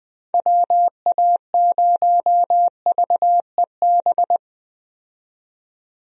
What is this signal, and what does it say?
Morse code, heard as a keyed tone, about 21 wpm, 700 Hz. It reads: WA0VEB
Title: USA Call Signs - 6 Characters - 20wpm